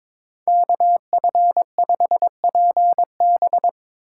Morse code, heard as a keyed tone, about 24 wpm, 700 Hz. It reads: KF5PB
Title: Speed Racing - USA 2x2 Call Signs 22wpm